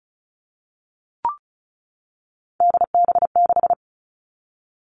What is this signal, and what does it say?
Morse code, heard as a keyed tone, about 34 wpm, 700 Hz. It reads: DB6
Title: All Cut Numbers - 3-Digit 35wpm